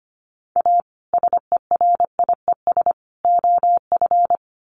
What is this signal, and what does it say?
Morse code, heard as a keyed tone, about 25 wpm, 700 Hz. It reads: A SERIES OF
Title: Sets of 3 Words 25wpm